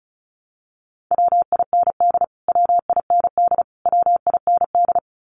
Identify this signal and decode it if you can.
Morse code, heard as a keyed tone, about 34 wpm, 700 Hz. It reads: WIND WIND WIND
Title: Top 500 Words Repeated 35wpm